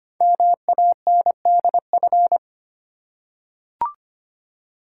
Morse code, heard as a keyed tone, about 25 wpm, 690 Hz. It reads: MANDF
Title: Code Groups - 5 Letters - 25wpm